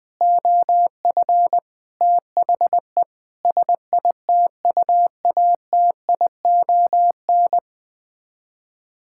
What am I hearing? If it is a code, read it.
Morse code, heard as a keyed tone, about 21 wpm, 700 Hz. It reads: OF THE SITUATION
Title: Sets of 9 Words - 20wpm